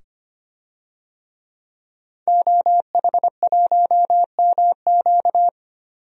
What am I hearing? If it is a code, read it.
Morse code, heard as a keyed tone, about 25 wpm, 700 Hz. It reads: OH1MQ
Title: DXCC-100 Call Signs - 5 Characters - 25wpm